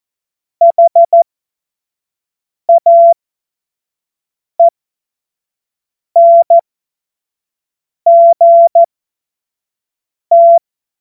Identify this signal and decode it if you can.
Morse code, heard as a keyed tone, about 14 wpm, spaced as for 5 wpm, 680 Hz. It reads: HAENGT
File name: Morsekurs / Witze / Witz 03. Januar 2023 25 F70